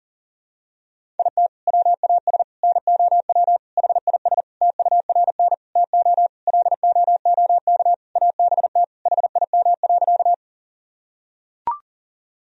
Morse code, heard as a keyed tone, about 37 wpm, 700 Hz. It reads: IT WAS NOW HIS TURN TO LOOK ABT HIM.
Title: Sentences from Top 200 Words and Abbreviations 40wpm